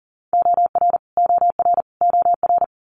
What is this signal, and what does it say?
Morse code, heard as a keyed tone, about 42 wpm, 700 Hz. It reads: OR OR OR
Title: Multi-letter Phonemes Repeated 3x 40wpm